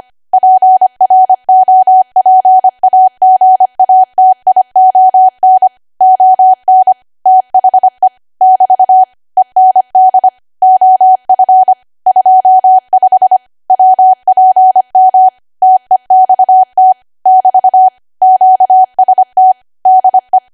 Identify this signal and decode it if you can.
Morse code, heard as a keyed tone, about 25 wpm, 750 Hz. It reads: PROPAGATION ON THE <BT> END OF 25 WPM TEXT <BT> QST DE